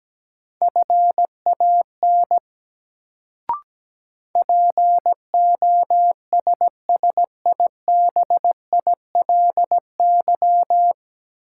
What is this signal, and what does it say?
Morse code, heard as a keyed tone, about 17 wpm, 700 Hz. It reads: FAN POSSIBILY
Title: Sentences from Top 1300 Words - New Word Drill Alt - 17wpm